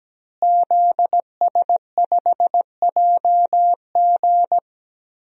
Morse code, heard as a keyed tone, about 17 wpm, 700 Hz. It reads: ZS5JG
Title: DXCC-100 Call Signs - 5 Characters - Repeated 3x - 17wpm